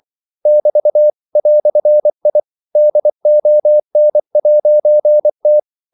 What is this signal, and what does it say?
Morse code, heard as a keyed tone, about 24 wpm, 600 Hz. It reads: <BT> "I DON'T
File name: CW / Practice / EnglishQuotations0324